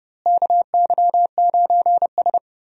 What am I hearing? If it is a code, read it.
Morse code, heard as a keyed tone, about 30 wpm, 700 Hz. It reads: KY9S
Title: USA Call Signs - 4 Characters - Repeated 2x - 30wpm